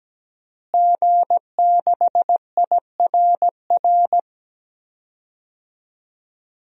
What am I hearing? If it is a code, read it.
Morse code, heard as a keyed tone, about 17 wpm, 700 Hz. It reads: G6IRR
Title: DXCC-100 Call Signs - 5 Characters - Repeated 3x - 17wpm